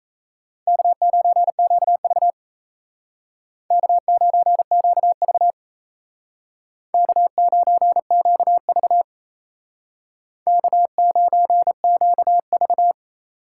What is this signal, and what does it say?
Morse code, heard as a keyed tone, about 35 wpm, 700 Hz. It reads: K9QV K9QV K9QV K9QV
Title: Speed Racing - USA Call Signs 28wpm